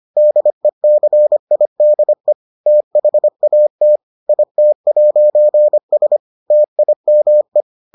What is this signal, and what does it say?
Morse code, heard as a keyed tone, about 25 wpm, 600 Hz. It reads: DECIDE THAT IT'S TIME